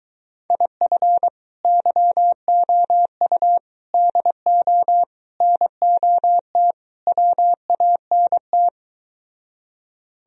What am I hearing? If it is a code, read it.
Morse code, heard as a keyed tone, about 22 wpm, 700 Hz. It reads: IF YOU DO NOT WANT